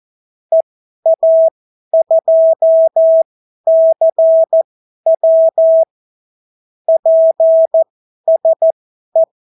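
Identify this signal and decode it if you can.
Morse code, heard as a keyed tone, about 14 wpm, spaced as for 11 wpm, 650 Hz. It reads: EA2CW PSE
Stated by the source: Example Spanish CW QSO from a Morse learning book, 14 wpm characters with 8 wpm spacing.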